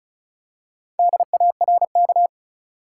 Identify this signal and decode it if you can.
Morse code, heard as a keyed tone, about 36 wpm, 700 Hz. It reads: DARK
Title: Four Letter Words - 35wpm